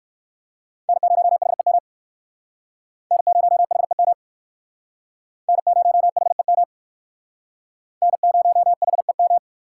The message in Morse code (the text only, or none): N0HEM N0HEM N0HEM N0HEM